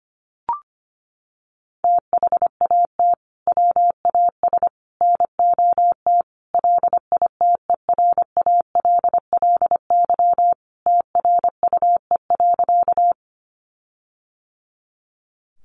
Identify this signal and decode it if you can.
Morse code, heard as a keyed tone, about 25 wpm, 700 Hz. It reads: THAT WAS NOT LITERALLY TRUE.
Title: Sentences from Top 1400 Words 25wpm